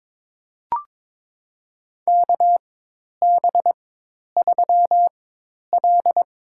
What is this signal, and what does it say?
Morse code, heard as a keyed tone, about 21 wpm, spaced as for 10 wpm, 700 Hz. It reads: KB3L
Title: USA Call Signs - 4x Character Spacing - 22wpm